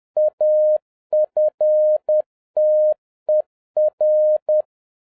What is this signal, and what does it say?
Morse code, heard as a keyed tone, about 10 wpm, 610 Hz. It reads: AFTER